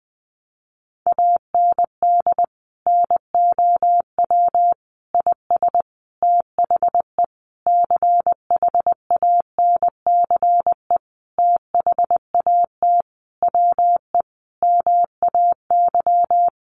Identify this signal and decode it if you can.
Morse code, heard as a keyed tone, about 20 wpm, 700 Hz. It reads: AND NOW IS THE CHANCE THAT WE MAY
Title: Sentences from Top 800 Words 20wpm